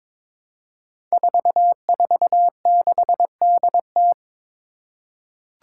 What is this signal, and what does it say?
Morse code, heard as a keyed tone, about 22 wpm, 700 Hz. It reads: 446DT
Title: Five Challenging Letter-Number Combinations 22wpm